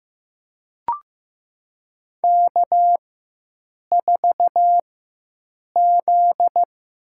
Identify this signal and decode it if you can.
Morse code, heard as a keyed tone, about 15 wpm, spaced as for 7 wpm, 700 Hz. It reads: K4Z
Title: USA Call Signs - 4 Characters - 4x Character Spacing - 15wpm